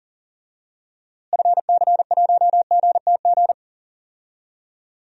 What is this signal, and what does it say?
Morse code, heard as a keyed tone, about 40 wpm, 700 Hz. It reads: FC1GTG
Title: DXCC-100 Call Signs - 6 Characters - Repeated 3x - 40wpm